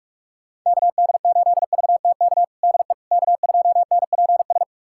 Morse code, heard as a keyed tone, about 46 wpm, 700 Hz. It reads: KD8VTK DE K2NPS
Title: USA Callsign DE Callsign 45wpm